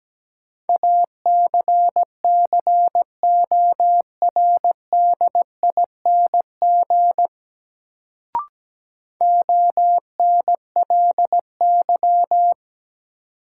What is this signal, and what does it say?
Morse code, heard as a keyed tone, about 17 wpm, 700 Hz. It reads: ACCORDING ONLY
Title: Sets of 2 Words - New Word Drill - 17wpm